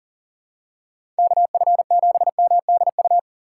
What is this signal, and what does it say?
Morse code, heard as a keyed tone, about 40 wpm, 700 Hz. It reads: KF7MDU
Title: USA Call Signs - 6 Characters - 6x Character Spacing - 40wpm